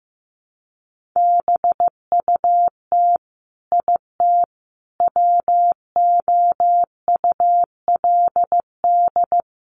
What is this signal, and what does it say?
Morse code, heard as a keyed tone, about 15 wpm, 700 Hz. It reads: BUT IT WOULD
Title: Sentences from Top 300 Words 15wpm